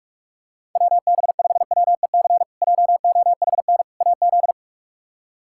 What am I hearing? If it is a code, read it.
Morse code, heard as a keyed tone, about 41 wpm, 700 Hz. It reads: WB5WEC JOHN AZ
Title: K1USN SST QSO Practice - 45wpm